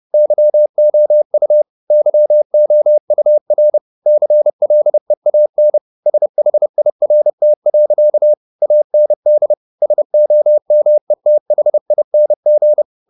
Morse code, heard as a keyed tone, about 30 wpm, 600 Hz. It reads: YOU YOUR CLEAN SHIRT. AND SOMETHING